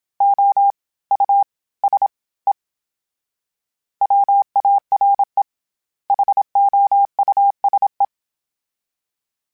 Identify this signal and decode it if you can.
Morse code, heard as a keyed tone, about 27 wpm, 800 Hz. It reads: O U S E WARE HOUSE
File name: General / General-mp3s / Houses @25